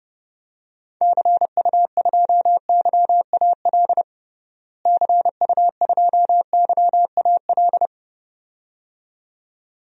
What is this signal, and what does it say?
Morse code, heard as a keyed tone, about 30 wpm, 700 Hz. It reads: CU2YAL CU2YAL
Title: DXCC-100 Call Signs - 6 Characters - Repeated 2x - 30wpm